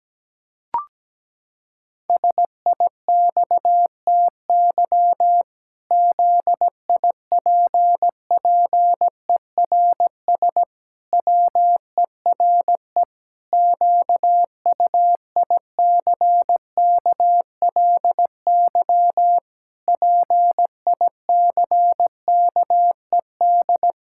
Morse code, heard as a keyed tone, about 17 wpm, 700 Hz. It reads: SIXTY ZIPPERS WERE QUICKLY PICKED
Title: Pangram Challenge v2 - 17wpm